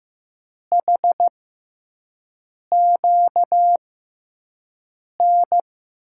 Morse code, heard as a keyed tone, about 15 wpm, spaced as for 5 wpm, 700 Hz. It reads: HQN
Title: USA Call Signs - 6x Character Spacing - 15wpm